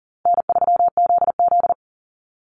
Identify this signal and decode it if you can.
Morse code, heard as a keyed tone, about 40 wpm, 700 Hz. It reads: N3ZZ
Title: Call Signs Repeated 3x - 40wpm